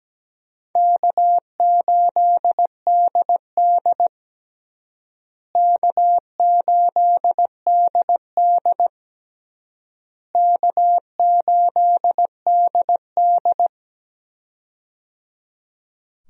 Morse code, heard as a keyed tone, about 17 wpm, 700 Hz. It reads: K8DD K8DD K8DD
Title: USA Call Signs - 4 Characters - Repeated 3x - 17wpm